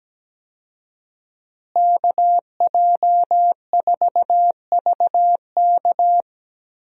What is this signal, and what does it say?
Morse code, heard as a keyed tone, about 17 wpm, 700 Hz. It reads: KJ4VK
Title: USA Call Signs - 5 Characters - Repeated 3x - 17wpm